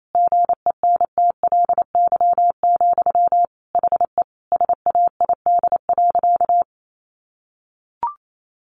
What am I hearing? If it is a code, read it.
Morse code, heard as a keyed tone, about 28 wpm, 700 Hz. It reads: GENTLY, HE SAID.
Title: Sentences from Top 1500 Words 28wpm